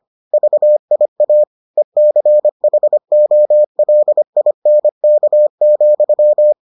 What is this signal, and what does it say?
Morse code, heard as a keyed tone, about 25 wpm, 600 Hz. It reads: VIA ECHOLINK,